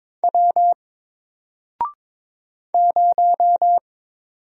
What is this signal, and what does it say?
Morse code, heard as a keyed tone, about 22 wpm, 700 Hz. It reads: W 0
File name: Single-Letter-Number-22wpm